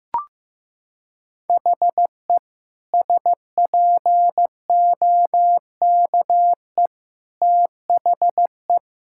Morse code, heard as a keyed tone, about 15 wpm, 700 Hz. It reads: HE SPOKE THE
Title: Sentences from Top 1600 Words 15wpm